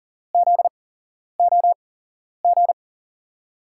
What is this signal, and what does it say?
Morse code, heard as a keyed tone, about 40 wpm, spaced as for 11 wpm, 700 Hz. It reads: ZOG
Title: USA Call Signs - 6 Characters - 8x Character Spacing - 40wpm